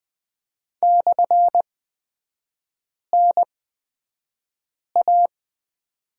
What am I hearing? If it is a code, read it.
Morse code, heard as a keyed tone, about 20 wpm, spaced as for 5 wpm, 700 Hz. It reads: /NA
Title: 250 Lesson 40e - BK - Three Characters 20wpm